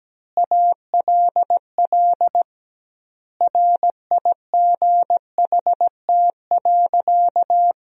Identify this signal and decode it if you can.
Morse code, heard as a keyed tone, about 17 wpm, 700 Hz. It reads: ALL RIGHT.
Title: 2x Word-Spacing - Sentences from Top 200 Words - 17wpm